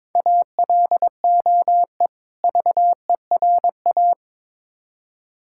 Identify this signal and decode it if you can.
Morse code, heard as a keyed tone, about 22 wpm, 700 Hz. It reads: ALOE VERA
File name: Difficult Words with Hints - 22wpm